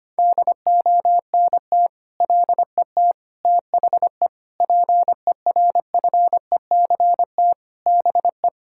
Morse code, heard as a keyed tone, about 25 wpm, 700 Hz. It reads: DONT LET THE PERFECT BE
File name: practice / 25wpm / Sentences with 9-Word Phrases - 25wpm